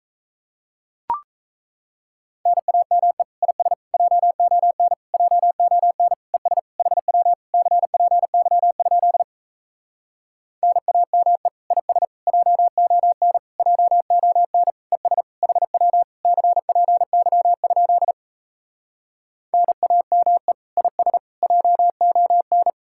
Morse code, heard as a keyed tone, about 37 wpm, 700 Hz. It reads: NAME IS JON JON ES HW CPY? NAME IS JON JON ES HW CPY? NAME IS JON